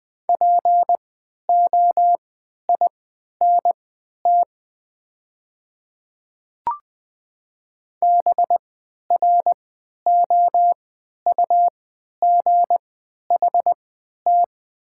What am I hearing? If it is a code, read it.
Morse code, heard as a keyed tone, about 20 wpm, spaced as for 11 wpm, 700 Hz. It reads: POINT BROUGHT
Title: 3x Character Spacing - Top 500 Words - 20wpm